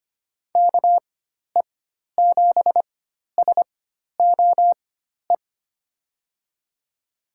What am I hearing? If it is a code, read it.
Morse code, heard as a keyed tone, about 25 wpm, spaced as for 12 wpm, 700 Hz. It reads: KE7SOE
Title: USA Call Signs - 6 Characters - 4x Character Spacing - 25wpm